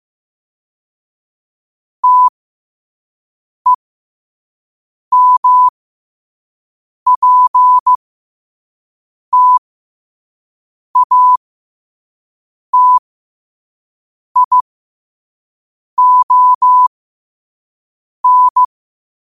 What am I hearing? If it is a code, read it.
Morse code, heard as a keyed tone, about 15 wpm, spaced as for 5 wpm, 1000 Hz. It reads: TEMPTATION
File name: qotdfiles / 5wpm